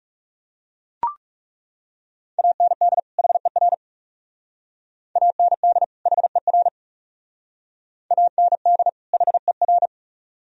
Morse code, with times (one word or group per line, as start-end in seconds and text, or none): 2.4-3.0 AND
3.2-3.7 HER
5.2-5.8 AND
6.1-6.7 HER
8.1-8.9 AND
9.1-9.9 HER